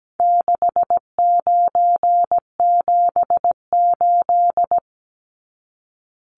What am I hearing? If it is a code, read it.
Morse code, heard as a keyed tone, about 17 wpm, 700 Hz. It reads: 6978
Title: Numbers - 4-Digit 17wpm